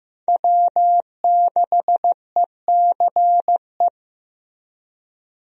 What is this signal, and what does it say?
Morse code, heard as a keyed tone, about 16 wpm, 700 Hz. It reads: W6ECE
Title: USA Call Signs - 5 Characters - Repeated 2x - 15wpm